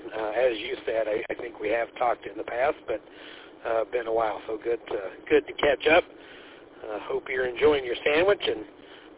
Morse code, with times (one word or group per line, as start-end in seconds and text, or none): none